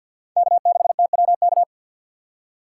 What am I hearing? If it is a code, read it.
Morse code, heard as a keyed tone, about 58 wpm, 700 Hz. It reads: K6TWK
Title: Speed Racing - USA 1x3 Call Signs 50wpm